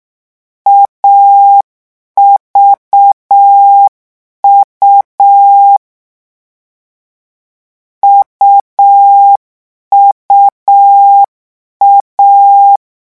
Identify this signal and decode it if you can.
Morse code, heard as a keyed tone, about 6 wpm, 800 Hz. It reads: AVU UUA
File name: morse mp3 / lecon3-vitesse 6